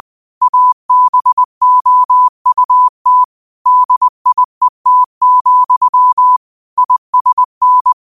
Morse code, heard as a keyed tone, about 20 wpm, 1000 Hz. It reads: ABOUT DIET, ISN